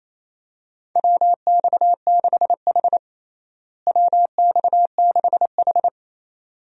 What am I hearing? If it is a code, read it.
Morse code, heard as a keyed tone, about 28 wpm, 700 Hz. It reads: WX6H WX6H
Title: USA Call Signs - 4 Characters - Repeated 2x - 28wpm